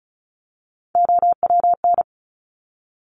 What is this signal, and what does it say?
Morse code, heard as a keyed tone, about 36 wpm, 700 Hz. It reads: OWN
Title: Top Three Letter Combinations in Words 35wpm